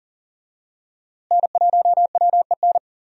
Morse code, heard as a keyed tone, about 40 wpm, 700 Hz. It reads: N1WEN
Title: USA Call Signs - 5 Characters - Repeated 2x - 40wpm